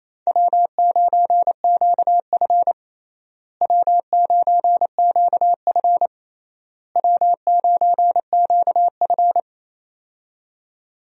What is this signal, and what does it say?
Morse code, heard as a keyed tone, about 28 wpm, 700 Hz. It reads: W9QF W9QF W9QF
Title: USA Call Signs - 4 Characters - Repeated 3x - 28wpm